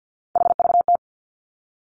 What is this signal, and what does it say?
Morse code, heard as a keyed tone, about 50 wpm, 700 Hz. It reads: HVT
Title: Three Challenging Letter-Number Combinations 50wpm